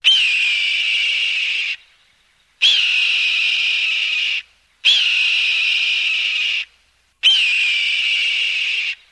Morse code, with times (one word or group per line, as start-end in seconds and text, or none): none